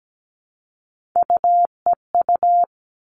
Morse code, heard as a keyed tone, about 17 wpm, 700 Hz. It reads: UEU
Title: All Cut Numbers - 3-Digit 17wpm